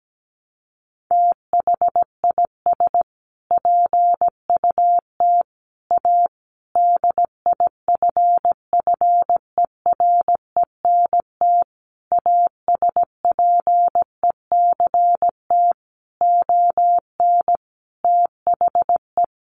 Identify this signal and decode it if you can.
Morse code, heard as a keyed tone, about 17 wpm, 700 Hz. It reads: THIS PUT A DIFFERENT ASPECT ON THE
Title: Sentences from Top 1300 Words 17wpm